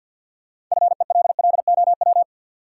Morse code, heard as a keyed tone, about 52 wpm, 700 Hz. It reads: FELLOW
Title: Sets of 5 Words Encore - New Word Drill - 50wpm